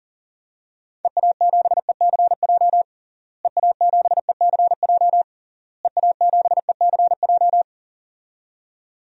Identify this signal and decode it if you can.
Morse code, heard as a keyed tone, about 40 wpm, 700 Hz. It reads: EA7ECJ EA7ECJ EA7ECJ
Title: DXCC-100 Call Signs - 6 Characters - Repeated 3x - 40wpm